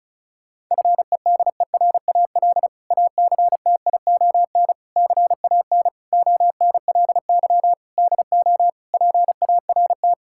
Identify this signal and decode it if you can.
Morse code, heard as a keyed tone, about 34 wpm, 700 Hz. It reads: FEDERAL ACTION CAN ONLY DO PART
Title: Sentences from Top 1000 Words - Review Part 5 - 35wpm